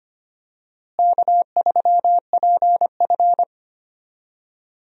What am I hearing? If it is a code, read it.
Morse code, heard as a keyed tone, about 25 wpm, 700 Hz. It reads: K3PF
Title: USA Call Signs - 4 Characters - Repeated 2x - 25wpm